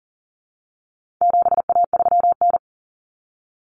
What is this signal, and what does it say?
Morse code, heard as a keyed tone, about 40 wpm, 700 Hz. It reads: 7A3N
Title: Common Cut Numbers - 4-Digit 40wpm